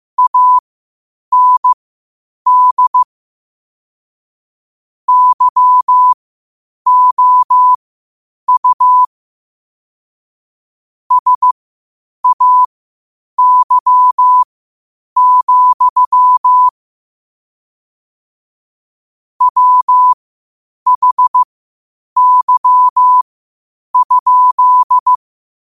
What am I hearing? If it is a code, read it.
Morse code, heard as a keyed tone, about 15 wpm, spaced as for 8 wpm, 1000 Hz. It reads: AND YOU SAY, WHY?
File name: qotdfiles / 8wpm